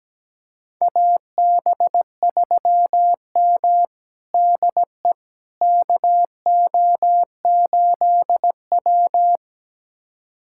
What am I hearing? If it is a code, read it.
Morse code, heard as a keyed tone, about 17 wpm, 700 Hz. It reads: AB3M DE KO8W
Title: USA Callsign DE Callsign - 4 Characters - 17wpm